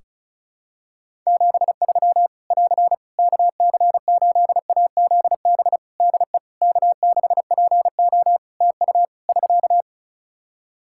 Morse code, heard as a keyed tone, about 36 wpm, 700 Hz. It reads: 73 <AR> KC8AZB DE K6PO TU <SK>
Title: Practice QSO - Exchange 4 - Ending - 35wpm